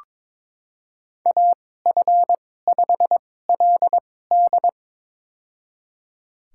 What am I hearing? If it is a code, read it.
Morse code, heard as a keyed tone, about 22 wpm, spaced as for 16 wpm, 700 Hz. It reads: AF5LD
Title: USA Call Signs - 2x Character Spacing - 22wpm